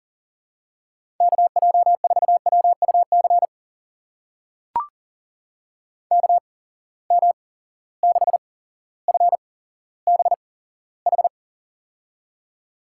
Morse code, heard as a keyed tone, about 40 wpm, spaced as for 17 wpm, 700 Hz. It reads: KJ4WUC KM6FBH
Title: USA Call Signs - 6 Characters - 8x Character Spacing - 40wpm